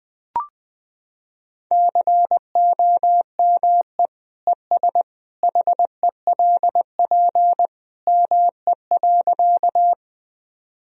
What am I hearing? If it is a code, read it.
Morse code, heard as a keyed tone, about 20 wpm, 700 Hz. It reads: COME ES HELP ME.
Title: Sentences from Top 200 Words and Abbreviations 20wpm